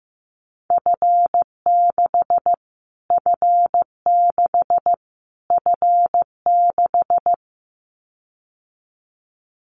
Morse code, heard as a keyed tone, about 15 wpm, 700 Hz. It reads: F6 F6 F6
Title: Callsign Prefixes for Top 25 DXCC - Repeated 3x - 15wpm